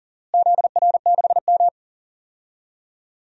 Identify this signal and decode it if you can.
Morse code, heard as a keyed tone, about 40 wpm, 700 Hz. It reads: ZR6M
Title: DXCC-100 Call Signs - 4 Characters - Repeated 2x - 40wpm